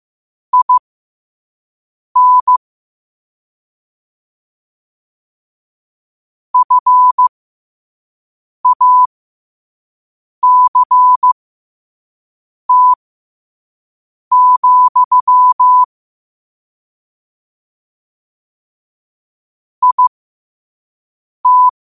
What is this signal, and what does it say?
Morse code, heard as a keyed tone, about 15 wpm, spaced as for 5 wpm, 1000 Hz. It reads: IN FACT, IT
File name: qotdfiles / 5wpm